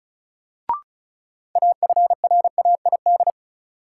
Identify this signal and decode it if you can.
Morse code, heard as a keyed tone, about 36 wpm, 700 Hz. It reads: AFRAID
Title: Sentences from Top 1500 Words - New Word Drill - 35wpm